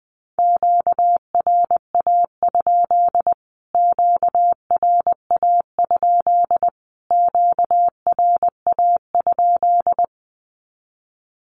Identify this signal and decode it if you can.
Morse code, heard as a keyed tone, about 20 wpm, 700 Hz. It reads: QRA? QRA? QRA?
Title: Q-Codes Repeated 3x 20wpm